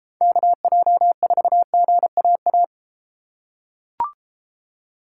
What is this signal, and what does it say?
Morse code, heard as a keyed tone, about 30 wpm, 700 Hz. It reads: KJ4GAA